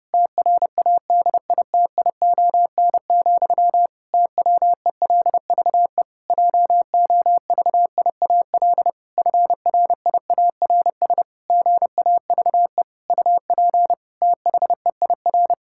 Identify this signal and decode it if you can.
Morse code, heard as a keyed tone, about 32 wpm, 700 Hz. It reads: TRADITION, TWELVE JOVIAL FRIARS GAVE UP THEIR
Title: Pangram Challenge v2 - 30wpm